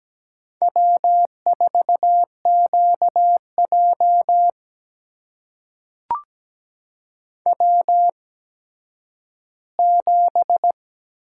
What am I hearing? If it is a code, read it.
Morse code, heard as a keyed tone, about 17 wpm, spaced as for 10 wpm, 700 Hz. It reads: W4QJ W7
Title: USA Call Signs - 4 Characters - 8x Character Spacing - 17wpm